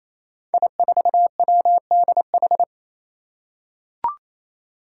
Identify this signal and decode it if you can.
Morse code, heard as a keyed tone, about 28 wpm, 700 Hz. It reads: I4WDH
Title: Five Challenging Letter-Number Combinations 28wpm